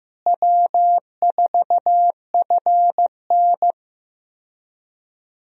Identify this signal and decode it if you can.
Morse code, heard as a keyed tone, about 15 wpm, 700 Hz. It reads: W4FN
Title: USA Call Signs - 4 Characters - Repeated 2x - 15wpm